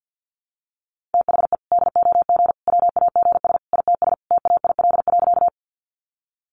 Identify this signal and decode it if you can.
Morse code, heard as a keyed tone, about 50 wpm, 700 Hz. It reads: THE DOG WAGS ITS TAIL.
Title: Easy Phrases 50wpm